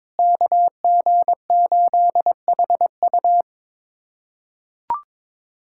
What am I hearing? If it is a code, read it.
Morse code, heard as a keyed tone, about 21 wpm, 700 Hz. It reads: KG8HU
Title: USA Callsign DE Callsign 22wpm